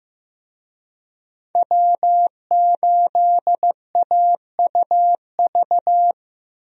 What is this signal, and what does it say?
Morse code, heard as a keyed tone, about 15 wpm, 700 Hz. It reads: W8AUV
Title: USA Call Signs - 5 Characters - Repeated 2x - 15wpm